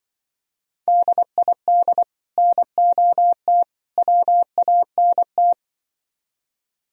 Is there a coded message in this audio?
Yes: Morse code from 0.9 to 5.5 s, heard as a keyed tone, about 25 wpm, 700 Hz. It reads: DID NOT WANT